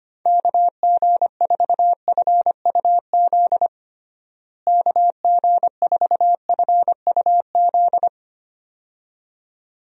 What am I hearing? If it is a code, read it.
Morse code, heard as a keyed tone, about 25 wpm, 700 Hz. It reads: KG4FUZ KG4FUZ
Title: USA Call Signs - 6 Characters - Repeated 2x - 25wpm